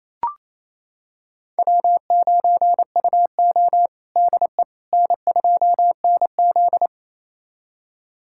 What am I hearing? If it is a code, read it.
Morse code, heard as a keyed tone, about 28 wpm, 700 Hz. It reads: W9UO DE N2NZ
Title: USA Callsign DE Callsign - 4 Characters - 28wpm